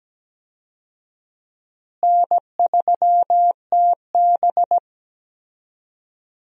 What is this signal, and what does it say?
Morse code, heard as a keyed tone, about 17 wpm, 700 Hz. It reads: N3TB
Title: USA Call Signs - 4 Characters - Repeated 3x - 17wpm